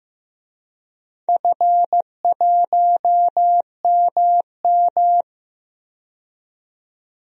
Morse code, heard as a keyed tone, about 15 wpm, 700 Hz. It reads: F1MM
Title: DXCC-100 Call Signs - 4 Characters - 15wpm